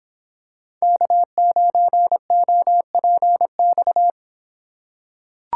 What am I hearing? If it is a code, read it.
Morse code, heard as a keyed tone, about 24 wpm, 700 Hz. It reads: K9OPX